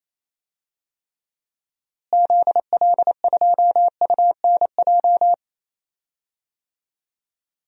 Morse code, heard as a keyed tone, about 28 wpm, 700 Hz. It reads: ZL2UNJ
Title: DXCC-100 Call Signs - 6 Characters - Repeated 3x - 28wpm